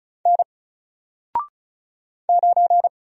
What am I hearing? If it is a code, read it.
Morse code, heard as a keyed tone, about 36 wpm, 700 Hz. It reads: N 9
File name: Single-Letter-Number-35wpm